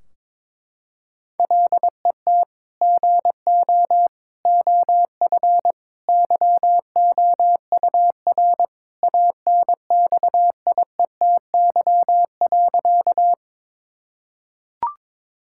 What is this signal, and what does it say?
Morse code, heard as a keyed tone, about 22 wpm, 700 Hz. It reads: LET GO OF YOUR ANXIETY.
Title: Sentences from Top 1600 Words 22wpm